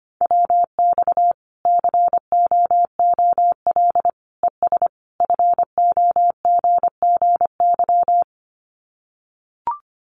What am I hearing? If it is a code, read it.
Morse code, heard as a keyed tone, about 25 wpm, 700 Hz. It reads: WX COOL ES FOGGY
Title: Common QSO Phrases 25wpm